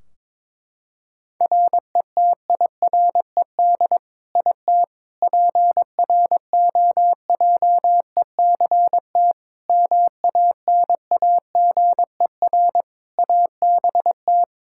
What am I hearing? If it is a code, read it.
Morse code, heard as a keyed tone, about 21 wpm, 700 Hz. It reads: RETIRED IT PROJECT MANAGER ABT
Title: Practice QSO - Exchange 3 - More Info - 22wpm